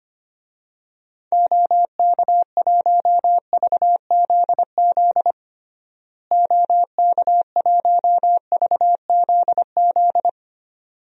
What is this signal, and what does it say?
Morse code, heard as a keyed tone, about 25 wpm, 700 Hz. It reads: OK1VZZ OK1VZZ
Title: DXCC-100 Call Signs - 6 Characters - Repeated 2x - 25wpm